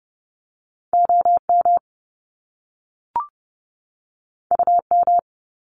Morse code, heard as a keyed tone, about 30 wpm, 700 Hz. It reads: OM UM
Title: Top Two Letter Combinations in Words 30wpm